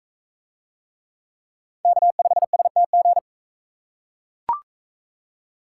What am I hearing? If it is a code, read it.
Morse code, heard as a keyed tone, about 39 wpm, 700 Hz. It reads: K5STG